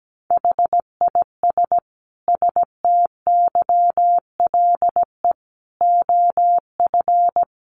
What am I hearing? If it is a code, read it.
Morse code, heard as a keyed tone, about 17 wpm, 700 Hz. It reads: HIS STYLE OF
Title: Sentences from Top 800 Words 17wpm